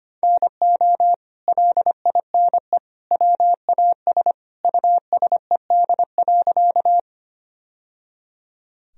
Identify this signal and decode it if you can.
Morse code, heard as a keyed tone, about 25 wpm, 700 Hz. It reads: NO LINE WAS USED.
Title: Sentences from Top 1000 Words - Review Part 1 - 25wpm